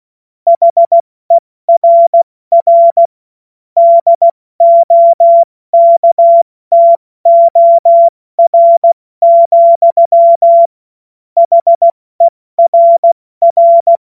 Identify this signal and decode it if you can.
Morse code, heard as a keyed tone, about 16 wpm, 680 Hz. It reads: HERR DOKTOR, HERR
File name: Morsekurs / Witze / Witz 04. Juli 2023 70 F80